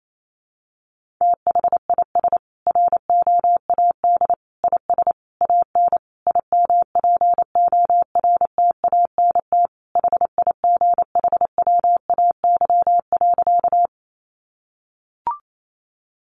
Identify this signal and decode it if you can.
Morse code, heard as a keyed tone, about 28 wpm, 700 Hz. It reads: THIS ROAD IS AN IMPORTANT HIGHWAY.
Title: Sentences from Top 1400 Words 28wpm